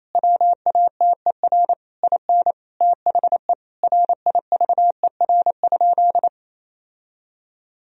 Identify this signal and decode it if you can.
Morse code, heard as a keyed tone, about 28 wpm, 700 Hz. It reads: WATER IN THE RIVER?
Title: Sentences from Top 1000 Words - Review Part 4 - 28wpm